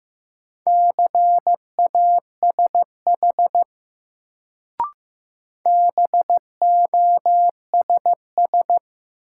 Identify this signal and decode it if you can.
Morse code, heard as a keyed tone, about 15 wpm, 700 Hz. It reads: CASH BOSS
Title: Sentences from Top 700 Words - New Word Drill - 15wpm